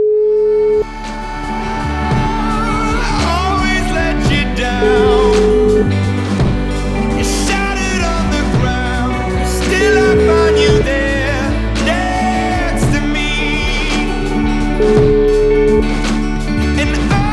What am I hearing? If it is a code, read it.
Ringback tone (Europe, 425 Hz), with noise.